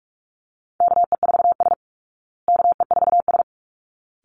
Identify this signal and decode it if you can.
Morse code, heard as a keyed tone, about 44 wpm, 700 Hz. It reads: KE4S KE4S
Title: Call Signs Repeated 3x - 45wpm